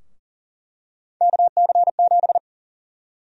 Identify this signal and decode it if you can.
Morse code, heard as a keyed tone, about 37 wpm, 700 Hz. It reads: KC7
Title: USA Call Sign Prefixes - 40wpm